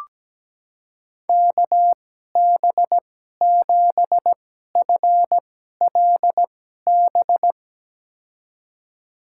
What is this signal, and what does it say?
Morse code, heard as a keyed tone, about 17 wpm, spaced as for 12 wpm, 700 Hz. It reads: KB7FLB
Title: USA Call Signs - 2x Character Spacing - 17wpm